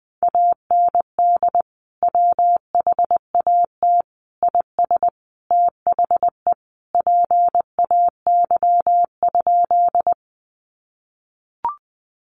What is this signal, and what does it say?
Morse code, heard as a keyed tone, about 20 wpm, 700 Hz. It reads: AND WHAT IS THE PAY?
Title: Sentences from Top 700 Words 20wpm